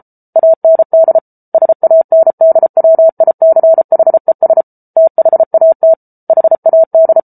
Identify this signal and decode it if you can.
Morse code, heard as a keyed tone, about 34 wpm, 650 Hz. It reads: AND SANDWICHES THAT HAD